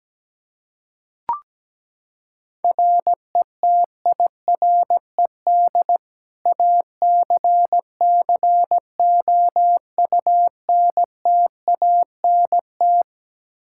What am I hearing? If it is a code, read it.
Morse code, heard as a keyed tone, about 17 wpm, 700 Hz. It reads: RETIRED ACCOUNTANT
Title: Practice QSO - Exchange 3 - More Info - 17wpm